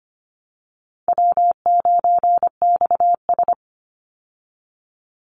Morse code, heard as a keyed tone, about 25 wpm, 700 Hz. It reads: W9XS
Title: Call Signs Repeated 3x - 25wpm